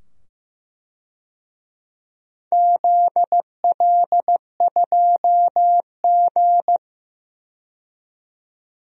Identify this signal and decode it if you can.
Morse code, heard as a keyed tone, about 15 wpm, 700 Hz. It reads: ZL2G
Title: DXCC-100 Call Signs - 4 Characters - 15wpm